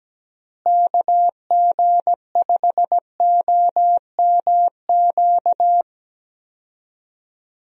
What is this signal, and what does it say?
Morse code, heard as a keyed tone, about 17 wpm, 700 Hz. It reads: KG5OMQ
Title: USA Call Signs - 6 Characters - Repeated 2x - 17wpm